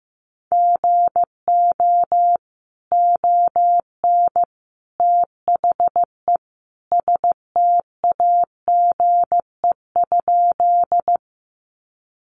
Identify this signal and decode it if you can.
Morse code, heard as a keyed tone, about 15 wpm, 700 Hz. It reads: GO ON THE STAGE?
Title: Sentences from Top 1200 Words 15wpm